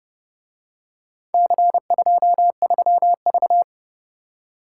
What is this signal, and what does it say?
Morse code, heard as a keyed tone, about 29 wpm, 700 Hz. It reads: C23V